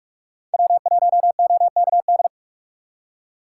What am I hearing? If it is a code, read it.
Morse code, heard as a keyed tone, about 44 wpm, 700 Hz. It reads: W1OKD
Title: USA Call Signs - 4x Character Spacing - 45wpm